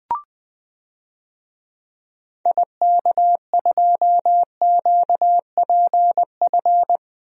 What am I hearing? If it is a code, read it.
Morse code, heard as a keyed tone, about 20 wpm, 700 Hz. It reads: IK2QPF
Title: DXCC-100 Call Signs - 6 Characters - Repeated 2x - 20wpm